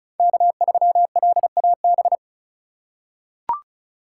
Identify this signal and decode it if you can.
Morse code, heard as a keyed tone, about 36 wpm, 700 Hz. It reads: K3LAB
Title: USA Call Signs - 5 Characters - 3x Character Spacing - 35wpm